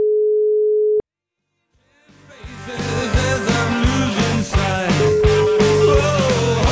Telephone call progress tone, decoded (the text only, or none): ringback tone